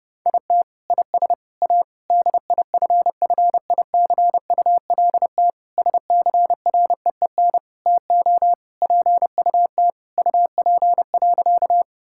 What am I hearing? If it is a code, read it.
Morse code, heard as a keyed tone, about 30 wpm, 700 Hz. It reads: IT IS A DIFFICULT SCREEN TO PUT UP.
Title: Sentences from Top 1000 Words - Review Part 6 - 30wpm